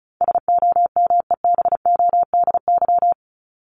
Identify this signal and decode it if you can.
Morse code, heard as a keyed tone, about 34 wpm, 700 Hz. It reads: SOMEBODY
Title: Top 1751-2000 Words 35wpm